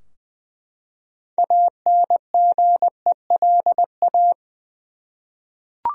Morse code, heard as a keyed tone, about 20 wpm, 700 Hz. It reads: ANGELA
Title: USA Common Names - 20wpm